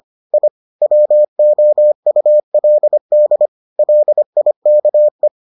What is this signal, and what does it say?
Morse code, heard as a keyed tone, about 25 wpm, 600 Hz. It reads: I WOULD LIKE